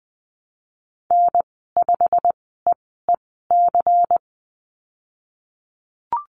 Morse code, heard as a keyed tone, about 20 wpm, spaced as for 15 wpm, 700 Hz. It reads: N5EEC
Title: 179 Lesson 29g - Slash - Call Signs 20wpm